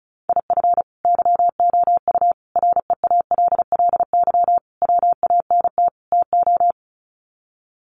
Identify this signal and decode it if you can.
Morse code, heard as a keyed tone, about 34 wpm, 700 Hz. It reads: IF YOU REALLY WANT TO
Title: Sets of 5 Words Encore2 35wpm